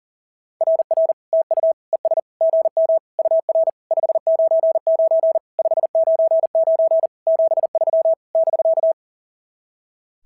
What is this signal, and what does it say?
Morse code, heard as a keyed tone, about 40 wpm, 690 Hz. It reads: RR TU ES GM UR 599 599 73 <BK>
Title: SOTA QSO Practice 40wpm